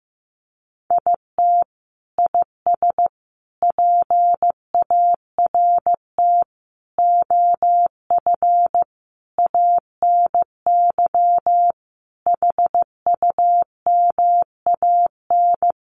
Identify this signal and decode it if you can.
Morse code, heard as a keyed tone, about 15 wpm, 700 Hz. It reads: IT IS PART OF ANY HUMAN
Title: Sentences from Top 1100 Words 15wpm